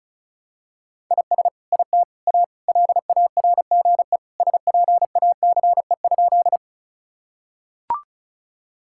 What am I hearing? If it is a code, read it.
Morse code, heard as a keyed tone, about 34 wpm, 700 Hz. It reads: IS IT A LARGE SPACE?
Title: Sentences from Top 1000 Words - Review Part 5 - 35wpm